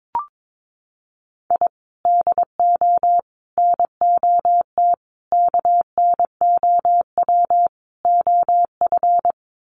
Morse code, heard as a keyed tone, about 22 wpm, 700 Hz. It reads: I DO NOT KNOW OF
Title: Sets of 5 Words Encore2 22wpm